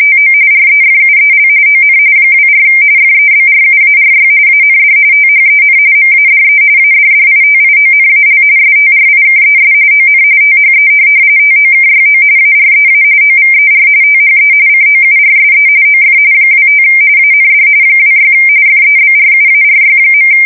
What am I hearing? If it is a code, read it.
RTTY, about 45.45 baud. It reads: TO REFUEL ON 24 FEBRUARY AND THEN MADE ADDITIONAL STRIKES ON THE TOKYO AREA OVER THE COURSE OF 25 AND 26 FEBRUARY.
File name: fsk170 / USS-Massachusetts